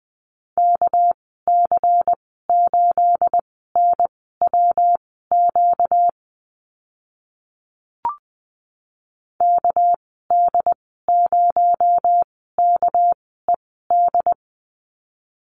Morse code, heard as a keyed tone, about 20 wpm, spaced as for 14 wpm, 700 Hz. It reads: KC8NWQ KD0KED
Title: 238 Lesson 38g - Letter q - Call Signs 20wpm